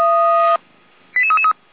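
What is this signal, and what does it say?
Five-tone selective call (ZVEI), heard as quick runs of five tones, tones of about 80 ms. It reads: 88333
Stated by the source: ZVEI five-tone selective call received by SDR, code 88199, recorded by feuerwehr-orlamuende.org fire brigade.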